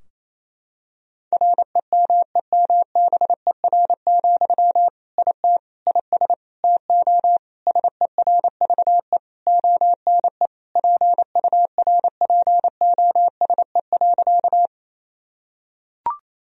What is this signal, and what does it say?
Morse code, heard as a keyed tone, about 28 wpm, 700 Hz. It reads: REMEMBER, IT IS TO SERVE ONE PURPOSE.
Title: Sentences from Top 1000 Words - Review Part 2 - 28wpm